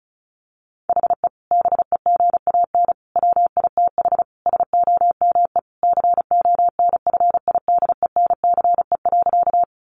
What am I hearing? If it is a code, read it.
Morse code, heard as a keyed tone, about 34 wpm, 700 Hz. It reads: HE BEGAN WITH SOME CONFIDENCE.
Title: Sentences from Top 1500 Words 35wpm